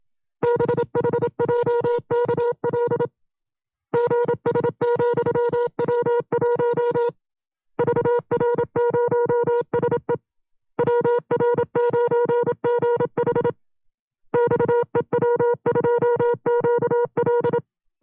Morse code, heard as a keyed tone, about 27 wpm, 480 Hz. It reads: BHJKL GS,W1 VR0SE WR9GH XEW2QL
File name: FreeFiles / CW-practice / 25WPM-side2